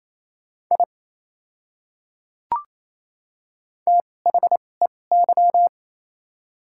Morse code, heard as a keyed tone, about 28 wpm, spaced as for 20 wpm, 700 Hz. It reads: I THEY
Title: 2x Character Spacing - Top 100 Words - 28wpm